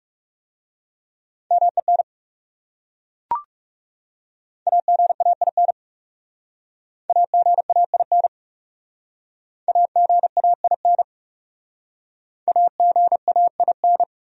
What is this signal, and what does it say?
Morse code, heard as a keyed tone, about 36 wpm, 700 Hz. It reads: MEN AGAIN AGAIN AGAIN AGAIN
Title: Speed-Racing Top 200 Words 30wpm